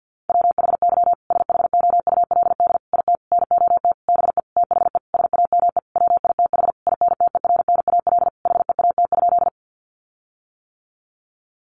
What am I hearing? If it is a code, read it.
Morse code, heard as a keyed tone, about 50 wpm, 700 Hz. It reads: WHY SHOULD IT NOT BE THE SAME WITH INTERNAL HEAT?
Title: Sentences from Top 1400 Words 50wpm